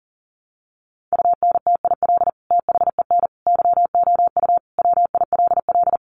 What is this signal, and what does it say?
Morse code, heard as a keyed tone, about 40 wpm, 700 Hz. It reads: UNTIL THEN YOU WILL
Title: Sentences from Top 600 Words 40wpm